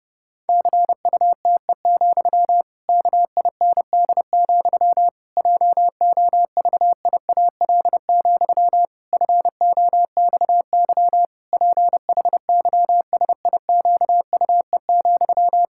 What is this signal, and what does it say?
Morse code, heard as a keyed tone, about 30 wpm, 700 Hz. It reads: CUTE, KIND, JOVIAL, FOXY PHYSIQUE,
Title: Pangram Challenge v2 - 30wpm